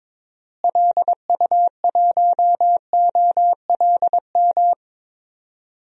Morse code, heard as a keyed tone, about 22 wpm, 700 Hz. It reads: LU1OLM
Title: DXCC-100 Call Signs - 6 Characters - Repeated 3x - 22wpm